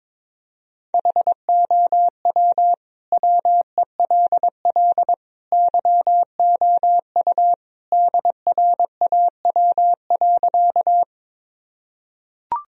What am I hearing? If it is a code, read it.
Morse code, heard as a keyed tone, about 22 wpm, 700 Hz. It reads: HOW WELL YOU DRAW.
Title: Sentences from Top 1000 Words - Review Part 1 - 22wpm